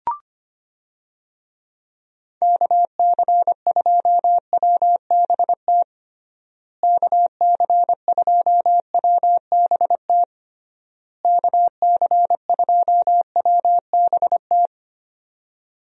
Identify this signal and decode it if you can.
Morse code, heard as a keyed tone, about 25 wpm, 700 Hz. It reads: KC2WBT KC2WBT KC2WBT
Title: USA Call Signs - 6 Characters - Repeated 3x - 25wpm